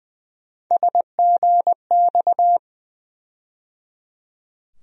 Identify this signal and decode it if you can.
Morse code, heard as a keyed tone, about 20 wpm, 700 Hz. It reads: SGX
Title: USA Call Sign Postfixes - 3 Letters 20wpm